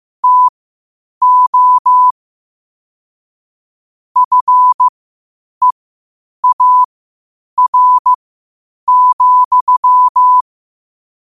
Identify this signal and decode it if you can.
Morse code, heard as a keyed tone, about 15 wpm, spaced as for 8 wpm, 1000 Hz. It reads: TO FEAR,
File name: qotdfiles / 8wpm